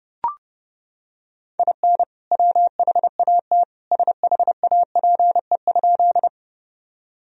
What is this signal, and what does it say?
Morse code, heard as a keyed tone, about 30 wpm, 700 Hz. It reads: IN WHAT SHAPE?
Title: Sentences from Top 1000 Words - Review Part 3 - 30wpm